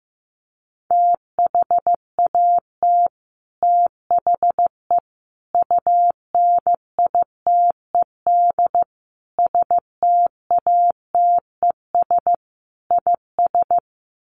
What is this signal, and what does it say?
Morse code, heard as a keyed tone, about 15 wpm, 700 Hz. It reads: THAT THE UNITED STATES IS
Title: Sets of 5 Words Encore 15wpm